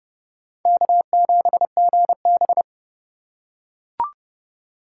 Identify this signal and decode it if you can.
Morse code, heard as a keyed tone, about 29 wpm, 700 Hz. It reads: K7GB